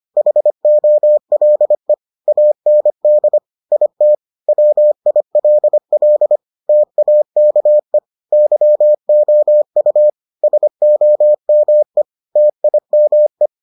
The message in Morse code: HOLE AND IT WILL TAKE YOU SOME TIME

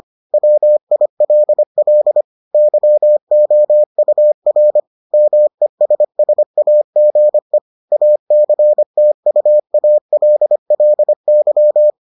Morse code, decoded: WILL YOUR MESSAGE ACTUALLY

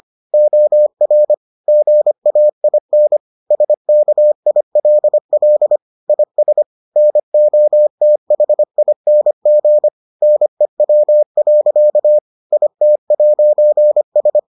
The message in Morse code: OR GAIN SKILL IS NOTHING NEW. IT'S

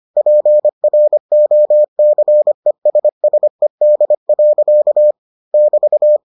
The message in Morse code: PROCESSED. <BT>

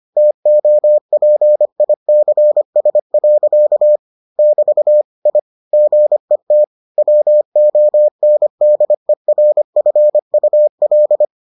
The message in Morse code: TOPICS. <BT> I GET WONDERFUL

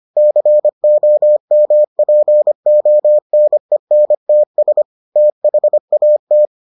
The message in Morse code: COMPONENTS THAT

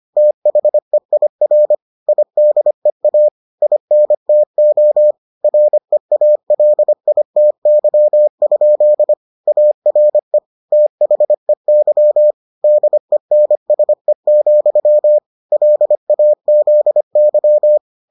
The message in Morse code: THEIR IDEA INTO REALITY? ARE THEY DENSE, LAZY

